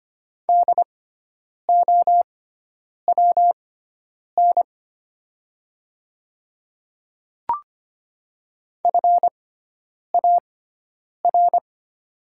Morse code, heard as a keyed tone, about 25 wpm, spaced as for 9 wpm, 700 Hz. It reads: DOWN FAR
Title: 6x Character Spacing - Top 500 Words - 25wpm